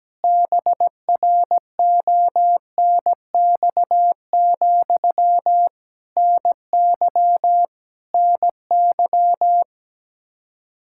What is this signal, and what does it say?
Morse code, heard as a keyed tone, about 17 wpm, 700 Hz. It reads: BRONX, NY NY
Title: Practice QSO - Exchange 1 - Beginning - 17wpm